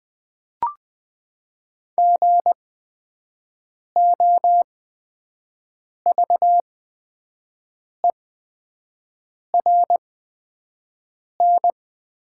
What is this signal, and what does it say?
Morse code, heard as a keyed tone, about 20 wpm, spaced as for 5 wpm, 700 Hz. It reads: GOVERN
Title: 8x Character Spacing - Top 500 Words - 20wpm